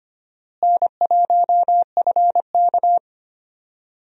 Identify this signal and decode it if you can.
Morse code, heard as a keyed tone, about 25 wpm, 700 Hz. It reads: N1FK
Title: USA Call Signs - 4 Characters - Repeated 3x - 25wpm